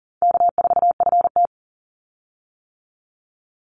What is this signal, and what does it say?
Morse code, heard as a keyed tone, about 40 wpm, 700 Hz. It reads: K4FT
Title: Easy Call Signs - 40wpm